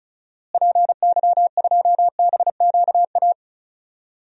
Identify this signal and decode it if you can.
Morse code, heard as a keyed tone, about 36 wpm, 700 Hz. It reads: PY2BQA
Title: DXCC-100 Call Signs - 6 Characters - Repeated 3x - 35wpm